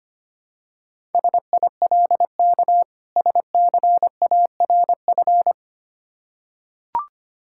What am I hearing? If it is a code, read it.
Morse code, heard as a keyed tone, about 25 wpm, 700 Hz. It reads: SILK SCARF
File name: practice / 25wpm / Difficult Words with Hints - 25wpm